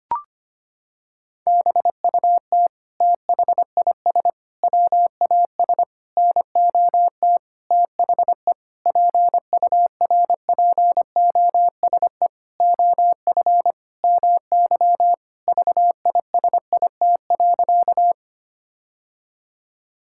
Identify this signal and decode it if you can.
Morse code, heard as a keyed tone, about 25 wpm, 700 Hz. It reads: BUT THIS WAS NOT THE PURPOSE OF MY VISIT.
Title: Sentences from Top 1000 Words - Review Part 6 - 25wpm